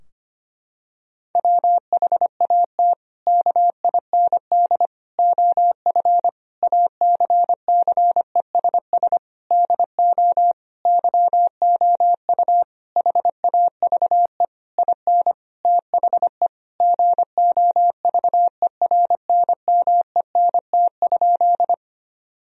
Morse code, heard as a keyed tone, about 25 wpm, 700 Hz. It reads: WHAT KIND OF ACCESS DO YOU HAVE IN THE GOVERNMENT?
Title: Sentences from Top 1000 Words - Review Part 4 - 25wpm